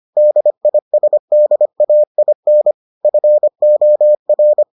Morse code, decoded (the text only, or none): DISDAIN FOR